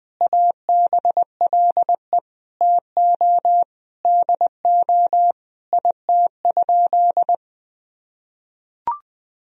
Morse code, heard as a keyed tone, about 21 wpm, 700 Hz. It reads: ABLE TO DO IT?
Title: Sentences from Top 1000 Words - Review Part 2 - 20wpm